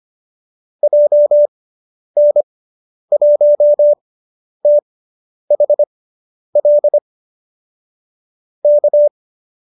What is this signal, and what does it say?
Morse code, heard as a keyed tone, about 25 wpm, spaced as for 10 wpm, 590 Hz. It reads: JN1THL K